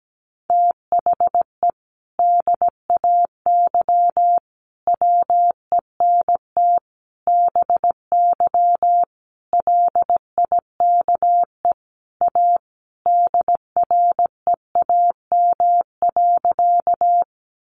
Morse code, heard as a keyed tone, about 17 wpm, 700 Hz. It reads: THE DAY WENT BY LIKE A DREAM.
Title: Sentences from Top 1500 Words 17wpm